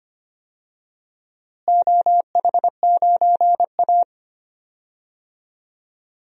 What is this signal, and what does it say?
Morse code, heard as a keyed tone, about 25 wpm, 700 Hz. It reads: OH9A
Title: DXCC-100 Call Signs - 4 Characters - Repeated 2x - 25wpm